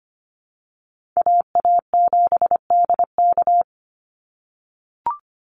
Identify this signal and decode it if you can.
Morse code, heard as a keyed tone, about 25 wpm, 700 Hz. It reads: AA7DK
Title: NA SOTA Chaser Callsigns 25wpm